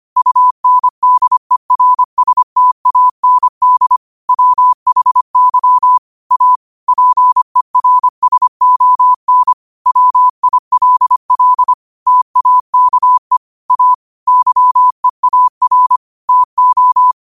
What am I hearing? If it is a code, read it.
Morse code, heard as a keyed tone, about 25 wpm, 1000 Hz. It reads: UNDERSTAND WHY A PERSON WILL TAKE A YEAR TO
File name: 25wpm